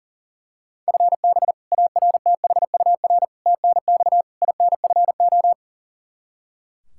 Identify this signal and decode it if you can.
Morse code, heard as a keyed tone, about 39 wpm, 700 Hz. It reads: FB ARTHUR TNX INFO
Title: Practice QSO - Exchange 3 - More Info - 40wpm